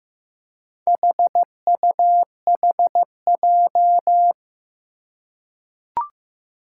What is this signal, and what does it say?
Morse code, heard as a keyed tone, about 15 wpm, 700 Hz. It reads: HUHJ
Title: Four Challenging Letter-Number Combinations 15wpm